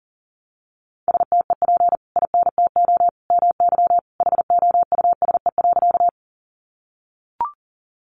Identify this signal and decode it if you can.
Morse code, heard as a keyed tone, about 40 wpm, 700 Hz. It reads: STEP INTO MY HOUSE.
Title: Sentences from Top 600 Words 40wpm